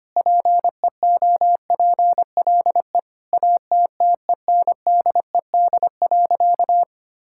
Morse code, heard as a keyed tone, about 25 wpm, 700 Hz. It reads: PEOPLE ATTENDED.
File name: practice / 25wpm / Sentences with 4-Word Phrases - 25wpm